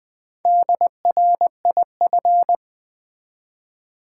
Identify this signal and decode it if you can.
Morse code, heard as a keyed tone, about 20 wpm, 700 Hz. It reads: DRIF
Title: Build A Word - 1 Letter At A Time - 5-Letter Words - 20wpm